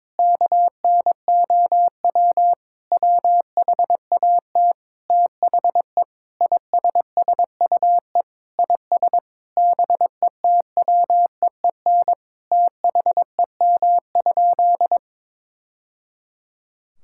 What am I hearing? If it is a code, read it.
Morse code, heard as a keyed tone, about 22 wpm, 700 Hz. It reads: KNOW WHAT THE ISSUE IS BETWEEN THEM?
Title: Sentences from Top 1000 Words - Review Part 6 - 22wpm